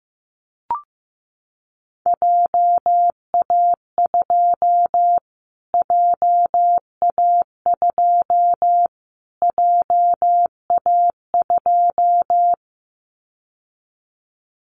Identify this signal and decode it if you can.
Morse code, heard as a keyed tone, about 15 wpm, 700 Hz. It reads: JA2 JA2 JA2
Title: Callsign Prefixes for Top 25 DXCC - Repeated 3x - 15wpm